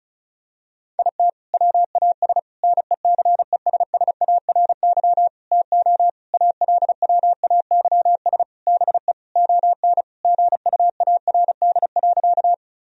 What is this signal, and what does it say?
Morse code, heard as a keyed tone, about 36 wpm, 700 Hz. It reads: IT WAS NECESSARY TO ALWAYS BE ON GUARD.
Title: Sentences from Top 1000 Words - Review Part 4 - 35wpm